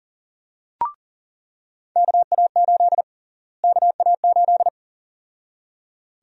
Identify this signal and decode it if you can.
Morse code, heard as a keyed tone, about 38 wpm, 700 Hz. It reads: KA8 KA8
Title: USA Call Sign Prefixes - Repeated 2x - 40wpm